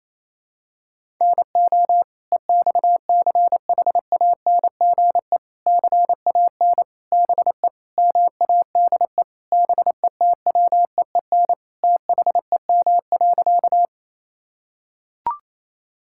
Morse code, heard as a keyed tone, about 28 wpm, 700 Hz. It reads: NO EXCHANGE CAN BE MADE BETWEEN THEM.
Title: Sentences from Top 1000 Words - Review Part 6 - 28wpm